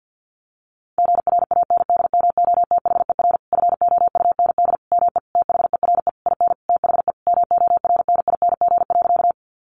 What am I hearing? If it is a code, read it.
Morse code, heard as a keyed tone, about 50 wpm, 700 Hz. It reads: GRANDMOTHER FOUND ME THERE IN THE MORNING.
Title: Sentences from Top 1200 Words 50wpm